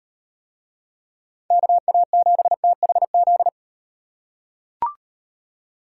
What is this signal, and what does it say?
Morse code, heard as a keyed tone, about 38 wpm, 700 Hz. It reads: KA7THZ